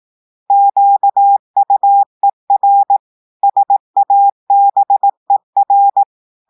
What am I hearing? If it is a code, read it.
Morse code, heard as a keyed tone, about 18 wpm, 800 Hz. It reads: QUER SABER
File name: portugais texte en clair 2 vitesse 18 mots minutes